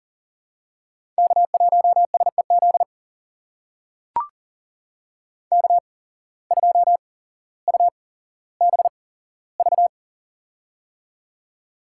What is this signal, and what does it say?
Morse code, heard as a keyed tone, about 40 wpm, spaced as for 17 wpm, 700 Hz. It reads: K1SEZ K2UBV
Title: USA Call Signs - 5 Characters - 8x Character Spacing - 40wpm